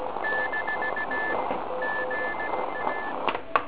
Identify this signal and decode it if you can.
Morse code, heard as a keyed tone, about 17 wpm, 1750 Hz, with noise. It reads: - GA
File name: ARCHIV / KX3 / PorovnaniKx3 RM40